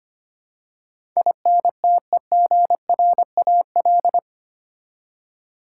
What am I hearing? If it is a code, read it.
Morse code, heard as a keyed tone, about 25 wpm, 700 Hz. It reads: INTEGRAL
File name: Eight Letter Words - 25wpm